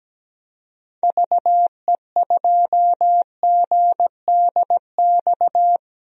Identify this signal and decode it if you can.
Morse code, heard as a keyed tone, about 17 wpm, 700 Hz. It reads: VE2GDX
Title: DXCC-100 Call Signs - 6 Characters - Repeated 3x - 17wpm